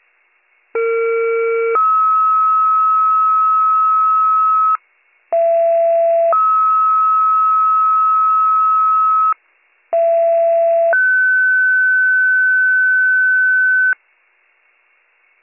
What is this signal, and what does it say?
Two-tone page, heard as a one-second tone, then a three-second tone, about 460 then 1290 Hz, about 670 then 1290 Hz, about 670 then 1500 Hz, with noise. It is heard from 0.7 to 13.9 s.